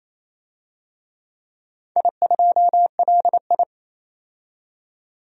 Morse code, heard as a keyed tone, about 28 wpm, 700 Hz. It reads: I2LI
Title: DXCC-100 Call Signs - 4 Characters - 28wpm